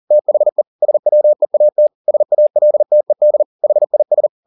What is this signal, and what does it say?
Morse code, heard as a keyed tone, about 40 wpm, 600 Hz. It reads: THE SWEAT SALTED HIS
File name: the old man and the sea 2 40wpm